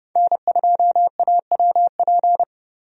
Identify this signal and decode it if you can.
Morse code, heard as a keyed tone, about 29 wpm, 700 Hz. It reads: N2AWP
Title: USA Call Signs - 5 Characters - 30wpm